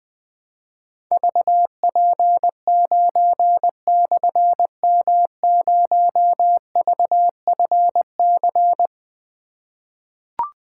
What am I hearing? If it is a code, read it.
Morse code, heard as a keyed tone, about 20 wpm, 700 Hz. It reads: VP9/M0VFC
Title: Difficult Call Signs 20wpm